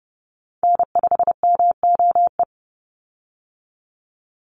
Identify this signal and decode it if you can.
Morse code, heard as a keyed tone, about 30 wpm, 700 Hz. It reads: N5MOE
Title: USA Call Signs 30wpm